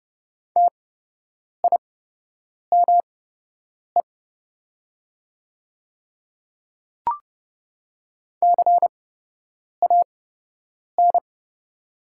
Morse code, heard as a keyed tone, about 30 wpm, spaced as for 8 wpm, 700 Hz. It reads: TIME CAN
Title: 8x Character Spacing - Top 100 Words - 30wpm